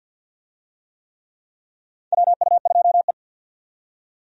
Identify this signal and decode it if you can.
Morse code, heard as a keyed tone, about 53 wpm, 700 Hz. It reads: WU2E
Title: USA Call Signs - 4 Characters - 50wpm